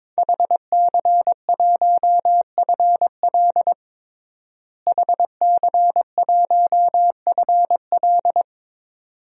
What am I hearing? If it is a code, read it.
Morse code, heard as a keyed tone, about 22 wpm, 700 Hz. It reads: HC1FL HC1FL
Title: DXCC-100 Call Signs - 5 Characters - Repeated 2x - 22wpm